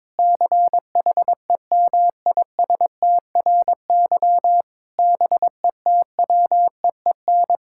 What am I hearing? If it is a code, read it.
Morse code, heard as a keyed tone, about 22 wpm, 700 Hz. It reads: CHEMISTRY BETWEEN
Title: Sentences from Top 1000 Words - Review Part 6 - 22wpm